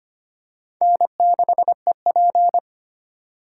Morse code, heard as a keyed tone, about 25 wpm, 700 Hz. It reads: N6EP
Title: USA Call Signs - 4 Characters - 4x Character Spacing - 25wpm